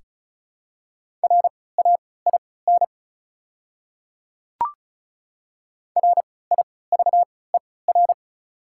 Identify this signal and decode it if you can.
Morse code, heard as a keyed tone, about 36 wpm, spaced as for 20 wpm, 700 Hz. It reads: RAIN RIVER
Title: 3x Character Spacing - Top 300 Words - 35wpm